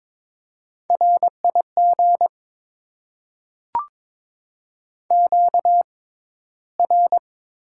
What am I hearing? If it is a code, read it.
Morse code, heard as a keyed tone, about 22 wpm, spaced as for 14 wpm, 700 Hz. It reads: RIG QR
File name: Top 20 QSO Elements - 6x Character Spacing - 22wpm